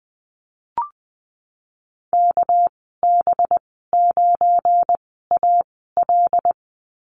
Morse code, heard as a keyed tone, about 20 wpm, spaced as for 15 wpm, 700 Hz. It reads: KB9AL
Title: 200 Lesson 32g - Letter k - Call Signs 20wpm